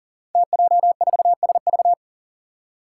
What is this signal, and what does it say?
Morse code, heard as a keyed tone, about 40 wpm, 700 Hz. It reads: TJ4SV
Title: Five Challenging Letter-Number Combinations 40wpm